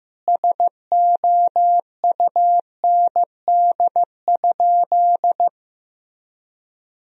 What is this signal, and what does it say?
Morse code, heard as a keyed tone, about 15 wpm, 700 Hz. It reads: SOUND?
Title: Sentences from Top 1000 Words - Review Part 1 - 15wpm